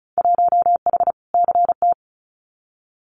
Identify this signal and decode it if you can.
Morse code, heard as a keyed tone, about 36 wpm, 700 Hz. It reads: 1H CT
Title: Winter Field Day Exchanges Encore 35wpm